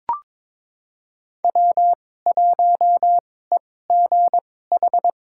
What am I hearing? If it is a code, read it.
Morse code, heard as a keyed tone, about 22 wpm, spaced as for 16 wpm, 700 Hz. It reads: W1EGH
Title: USA Call Signs - 2x Character Spacing - 22wpm